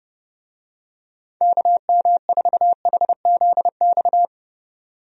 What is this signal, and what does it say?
Morse code, heard as a keyed tone, about 32 wpm, 700 Hz. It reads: KM4HZX
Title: USA Call Signs - 6 Characters - 30wpm